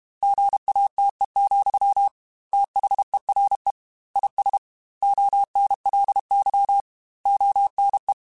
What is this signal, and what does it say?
Morse code, heard as a keyed tone, about 32 wpm, 800 Hz. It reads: GATE, THERE IS ONLY ONE
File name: Technical Subjects / Technical Subjects-mp3s / OR Gate @30